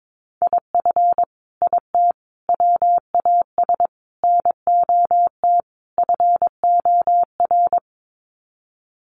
Morse code, heard as a keyed tone, about 22 wpm, 700 Hz. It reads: IF IT WAS NOT FOR
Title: Sets of 5 Words Encore 22wpm